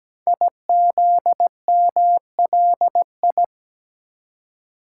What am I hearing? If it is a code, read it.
Morse code, heard as a keyed tone, about 17 wpm, 700 Hz. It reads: IZMLI
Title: Code Groups - 5 Letters - 17wpm